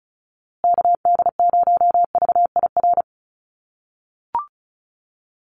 Morse code, heard as a keyed tone, about 34 wpm, 700 Hz. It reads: KD0VIR
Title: USA Call Signs Repeated 3x 35wpm